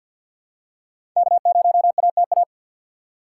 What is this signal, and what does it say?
Morse code, heard as a keyed tone, about 50 wpm, 700 Hz. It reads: K0ATA
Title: USA Call Signs - 5 Characters - 2x Character Spacing - 50wpm